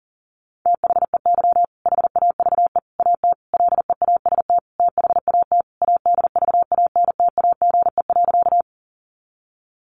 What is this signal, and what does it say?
Morse code, heard as a keyed tone, about 40 wpm, 700 Hz. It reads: THEY HAVE AT LEAST THAT ADVANTAGE.
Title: Sentences from Top 900 Words 40wpm